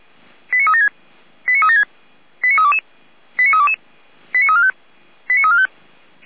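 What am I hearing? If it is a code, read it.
Five-tone selective call (ZVEI), heard as quick runs of five tones, tones of about 80 ms. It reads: 89376 89376 89322 89322 89345 89345
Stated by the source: ZVEI five-tone selective call received on SDR, recorded by Feuerwehr Orlamünde (fire brigade).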